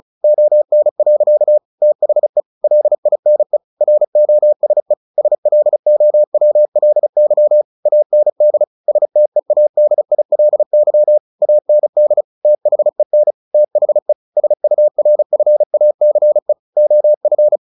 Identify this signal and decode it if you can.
Morse code, heard as a keyed tone, about 34 wpm, 600 Hz. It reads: ON. THE LINE ROSE SLOWLY AND STEADILY AND THEN THE SURFACE OF
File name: the old man and the sea / the old man and the sea 1 35wpm